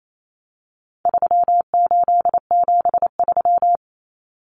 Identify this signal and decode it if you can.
Morse code, heard as a keyed tone, about 28 wpm, 700 Hz. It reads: 3873
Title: Numbers - 4-Digit 28wpm